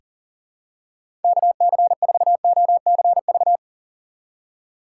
Morse code, heard as a keyed tone, about 40 wpm, 700 Hz. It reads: KC4OCV
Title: USA Call Signs - 6 Characters - 6x Character Spacing - 40wpm